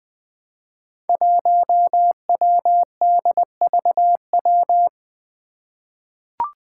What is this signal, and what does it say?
Morse code, heard as a keyed tone, about 21 wpm, 700 Hz. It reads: 1WDVW
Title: Five Challenging Letter-Number Combinations 20wpm